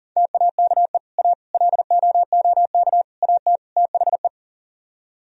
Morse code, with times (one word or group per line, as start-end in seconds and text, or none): 0.2-1.0 TAKE
1.2-1.3 A
1.5-3.0 LOOK
3.2-3.6 AT
3.8-4.3 THE